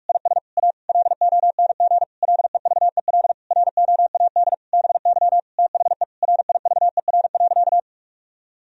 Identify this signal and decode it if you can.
Morse code, heard as a keyed tone, about 41 wpm, 700 Hz. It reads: IS A LONG LEVEL ROAD BY THE RIVER.
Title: Sentences from Top 1000 Words - Review Part 4 - 45wpm